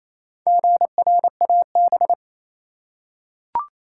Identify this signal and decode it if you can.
Morse code, heard as a keyed tone, about 28 wpm, 700 Hz. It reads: GRAB
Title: Four Letter Words - 28wpm